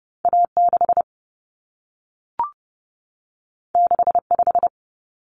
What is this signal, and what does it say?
Morse code, heard as a keyed tone, about 30 wpm, 700 Hz. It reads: A6 65
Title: Common Cut Numbers - 2-Digit 30wpm